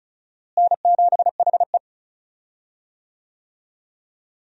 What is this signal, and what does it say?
Morse code, heard as a keyed tone, about 36 wpm, 700 Hz. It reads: N7HE
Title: USA Call Signs - 4 Characters - Repeated 2x - 35wpm